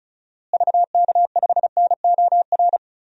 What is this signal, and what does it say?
Morse code, heard as a keyed tone, about 34 wpm, 700 Hz. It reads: VK5NOR
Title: DXCC-100 Call Signs - 6 Characters - Repeated 3x - 35wpm